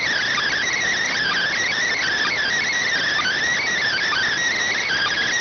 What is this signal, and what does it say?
Emergency Alert System (SAME) tones, heard as harsh buzzing bursts. It is heard from 1.1 to 4.3 s.